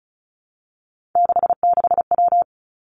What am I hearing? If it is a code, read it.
Morse code, heard as a keyed tone, about 36 wpm, 700 Hz. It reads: 66W
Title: Three Challenging Letter-Number Combinations 35wpm